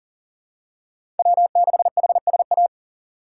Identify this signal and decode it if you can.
Morse code, heard as a keyed tone, about 40 wpm, 700 Hz. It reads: W6HSA
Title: USA Call Signs - 5 Characters - 4x Character Spacing - 40wpm